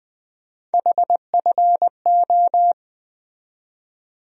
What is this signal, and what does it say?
Morse code, heard as a keyed tone, about 20 wpm, 700 Hz. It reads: HFO
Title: USA Call Sign Postfixes - 3 Letters 20wpm